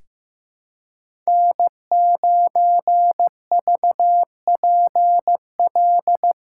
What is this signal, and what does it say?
Morse code, heard as a keyed tone, about 16 wpm, 700 Hz. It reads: N9VPL
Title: USA Callsign DE Callsign - 5 Characters - 15wpm